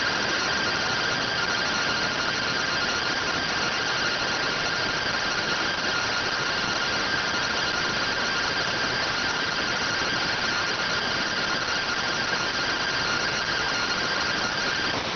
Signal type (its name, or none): two-tone teleprinter signal